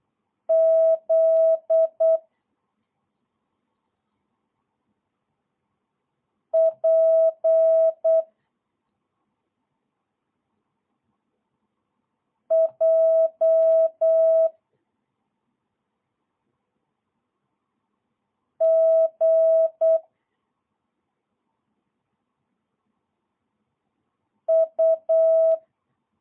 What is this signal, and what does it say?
Morse code, heard as a keyed tone, about 8 wpm, 650 Hz. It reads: Z P J G U